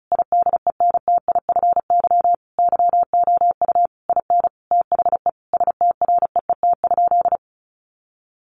Morse code, heard as a keyed tone, about 34 wpm, 700 Hz. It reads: IDENTIFY YOU IN THE STREET?
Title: Sentences from Top 900 Words 35wpm